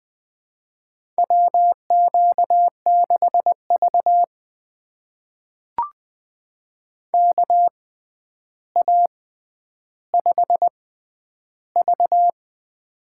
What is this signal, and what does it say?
Morse code, heard as a keyed tone, about 20 wpm, spaced as for 10 wpm, 700 Hz. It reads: WQ6V KA5V
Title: USA Call Signs - 4 Characters - 6x Character Spacing - 20wpm